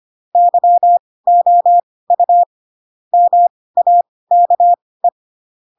Morse code, heard as a keyed tone, about 25 wpm, spaced as for 18 wpm, 700 Hz. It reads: YOU MAKE
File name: Pinocchio / Pinocchio ch2 12 25 ik1hge